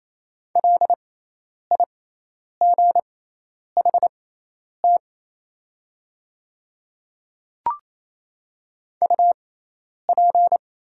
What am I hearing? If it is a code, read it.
Morse code, heard as a keyed tone, about 28 wpm, spaced as for 10 wpm, 700 Hz. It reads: LIGHT UP
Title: 6x Character Spacing - Top 200 Words - 28wpm